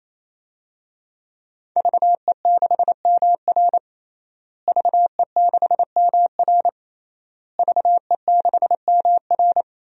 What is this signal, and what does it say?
Morse code, heard as a keyed tone, about 28 wpm, 700 Hz. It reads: VE6MR VE6MR VE6MR
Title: DXCC-100 Call Signs - 5 Characters - Repeated 3x - 28wpm